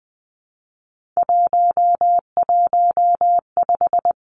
Morse code, heard as a keyed tone, about 20 wpm, 700 Hz. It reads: 115
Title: Three Challenging Letter-Number Combinations 20wpm